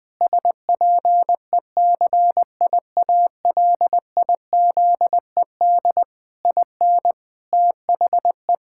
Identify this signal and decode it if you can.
Morse code, heard as a keyed tone, about 20 wpm, 700 Hz. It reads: SPECIALIZED IN THE
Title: Pangram Challenge v2 - 20wpm